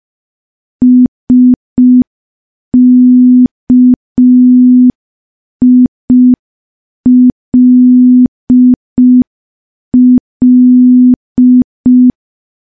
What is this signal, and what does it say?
Morse code, heard as a keyed tone, about 5 wpm, 250 Hz. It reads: SKILL
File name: morse / morse-training-05WPM